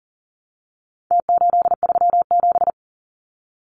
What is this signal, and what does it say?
Morse code, heard as a keyed tone, about 40 wpm, 700 Hz. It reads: T837
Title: Numbers - Contest Style - 4-Digit 40wpm